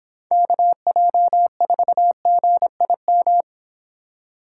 Morse code, heard as a keyed tone, about 26 wpm, 700 Hz. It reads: KJ4GIM